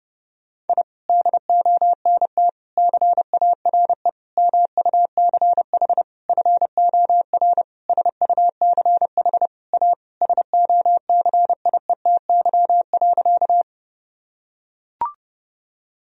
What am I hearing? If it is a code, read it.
Morse code, heard as a keyed tone, about 30 wpm, 700 Hz. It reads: I DONT CARE MUCH FOR SUCH A SOCIETY.
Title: Sentences from Top 1000 Words - Review Part 4 - 30wpm